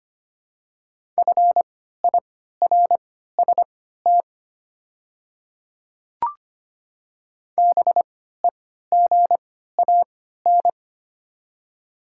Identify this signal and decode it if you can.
Morse code, heard as a keyed tone, about 25 wpm, spaced as for 14 wpm, 700 Hz. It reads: FIRST BEGAN
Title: 3x Character Spacing - Top 300 Words - 25wpm